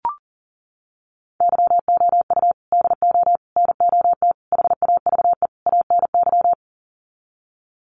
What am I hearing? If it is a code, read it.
Morse code, heard as a keyed tone, about 40 wpm, 700 Hz. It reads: YOU DO NOT HAVE ANY
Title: Sets of 5 Words Encore 40wpm